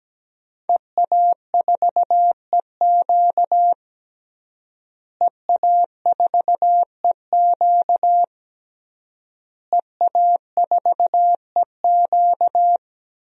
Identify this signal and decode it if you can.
Morse code, heard as a keyed tone, about 17 wpm, 700 Hz. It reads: EA4EQ EA4EQ EA4EQ
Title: DXCC-100 Call Signs - 5 Characters - Repeated 3x - 17wpm